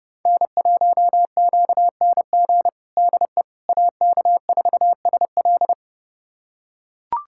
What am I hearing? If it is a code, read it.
Morse code, heard as a keyed tone, about 30 wpm, 700 Hz. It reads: N1QNG DE AK4SL
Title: USA Callsign DE Callsign - 5 Characters - 30wpm